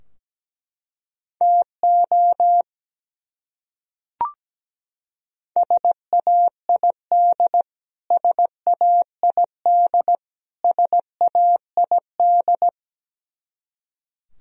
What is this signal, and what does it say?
Morse code, heard as a keyed tone, about 17 wpm, 700 Hz. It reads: TO SAID SAID SAID
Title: Top 100 Words Repeated 17wpm